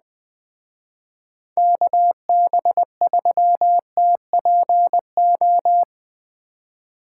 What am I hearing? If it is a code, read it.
Morse code, heard as a keyed tone, about 20 wpm, 700 Hz. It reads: KB3TPO
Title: Speed Racing - USA 2x3 Call Signs 17wpm